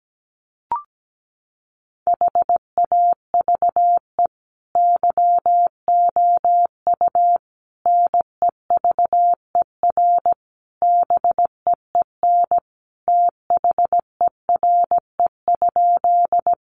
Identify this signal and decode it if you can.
Morse code, heard as a keyed tone, about 17 wpm, 700 Hz. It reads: HAVE YOU NEVER BEEN THERE?
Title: Sentences from Top 300 Words 17wpm